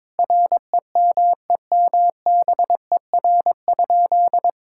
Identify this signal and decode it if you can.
Morse code, heard as a keyed tone, about 22 wpm, 700 Hz. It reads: REMEMBER?
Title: Sentences from Top 1000 Words - Review Part 6 - 22wpm